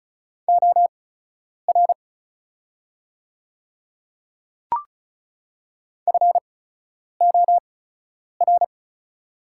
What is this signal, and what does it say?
Morse code, heard as a keyed tone, about 36 wpm, spaced as for 10 wpm, 700 Hz. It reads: OR FOR
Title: 8x Character Spacing - Top 100 Words - 35wpm